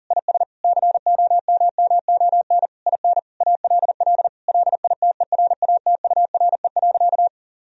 Morse code, heard as a keyed tone, about 40 wpm, 700 Hz. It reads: IS COMMON IN ALL LITERATURE.
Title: Sentences from Top 1000 Words - Review Part 3 - 40wpm